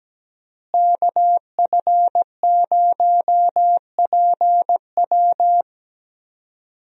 Morse code, heard as a keyed tone, about 17 wpm, 700 Hz. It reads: KF0PW
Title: USA Call Signs - 5 Characters - Repeated 3x - 17wpm